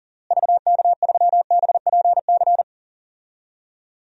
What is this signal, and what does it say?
Morse code, heard as a keyed tone, about 37 wpm, 700 Hz. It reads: VK3BPC
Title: DXCC-100 Call Signs - 6 Characters - Repeated 3x - 40wpm